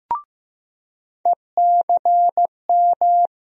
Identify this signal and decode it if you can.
Morse code, heard as a keyed tone, about 15 wpm, 700 Hz. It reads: ECM
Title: USA Call Sign Postfixes - 3 Letters 15wpm